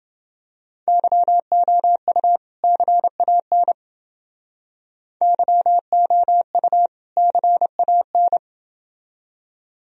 Morse code, heard as a keyed tone, about 29 wpm, 700 Hz. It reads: YOU CAN YOU CAN